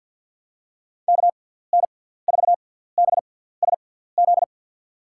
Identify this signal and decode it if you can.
Morse code, heard as a keyed tone, about 52 wpm, spaced as for 17 wpm, 700 Hz. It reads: KN4BSZ
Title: USA Call Signs - 6x Character Spacing - 50wpm